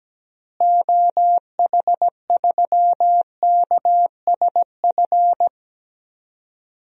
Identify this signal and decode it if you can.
Morse code, heard as a keyed tone, about 17 wpm, 700 Hz. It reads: OH3KSF
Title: DXCC-100 Call Signs - 6 Characters - Repeated 2x - 17wpm